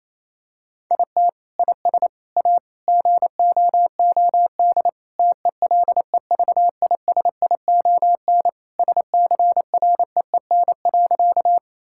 Morse code, heard as a keyed tone, about 28 wpm, 700 Hz. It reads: IT IS A GOOD TELEVISION SCREEN.
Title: Sentences from Top 1000 Words - Review Part 6 - 28wpm